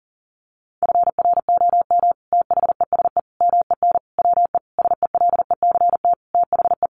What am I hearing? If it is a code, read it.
Morse code, heard as a keyed tone, about 40 wpm, 700 Hz. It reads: FROM THESE MEN WE SELECT THE
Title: Sentences from Top 1300 Words 40wpm